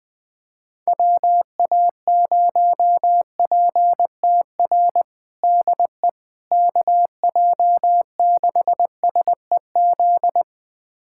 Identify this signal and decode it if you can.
Morse code, heard as a keyed tone, about 19 wpm, 700 Hz. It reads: WA0PTR DE KJ6SEZ
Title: USA Callsign DE Callsign - 6 Characters - 20wpm